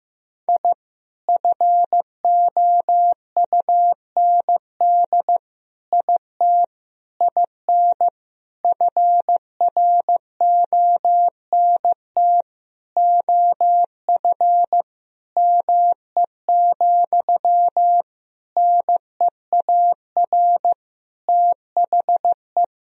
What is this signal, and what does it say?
Morse code, heard as a keyed tone, about 15 wpm, 700 Hz. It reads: I FOUND IT IN FRONT OF ME, NEAR THE
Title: Sentences from Top 1000 Words - Review Part 2 - 15wpm